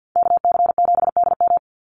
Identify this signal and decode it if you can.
Morse code, heard as a keyed tone, about 50 wpm, 700 Hz. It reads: KC7DM
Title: NA SOTA Activator Callsigns 50wpm